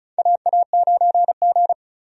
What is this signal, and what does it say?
Morse code, heard as a keyed tone, about 34 wpm, 700 Hz. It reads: AA9G
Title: USA Call Signs - 4 Characters - 35wpm